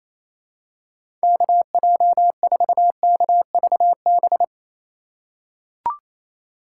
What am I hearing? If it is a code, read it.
Morse code, heard as a keyed tone, about 28 wpm, 700 Hz. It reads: KJ4KVB
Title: USA Call Signs - 6 Characters - 2x Character Spacing - 28wpm